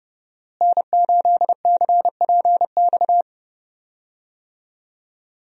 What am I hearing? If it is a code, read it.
Morse code, heard as a keyed tone, about 30 wpm, 700 Hz. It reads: N8CPX
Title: USA Call Signs - 5 Characters - Repeated 3x - 30wpm